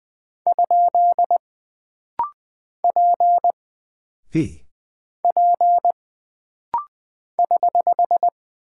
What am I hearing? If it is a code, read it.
Morse code, heard as a keyed tone, about 20 wpm, 700 Hz. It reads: ? P P <HH>
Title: Single Character (Letter:Number:Punctuation:Prosign) - Rapid-Fire - 20wpm